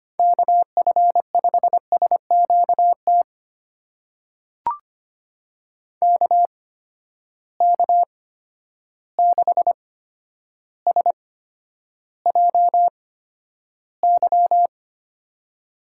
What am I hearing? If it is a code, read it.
Morse code, heard as a keyed tone, about 25 wpm, spaced as for 11 wpm, 700 Hz. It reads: KF5SQT KK6SJY
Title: USA Call Signs - 6 Characters - 8x Character Spacing - 25wpm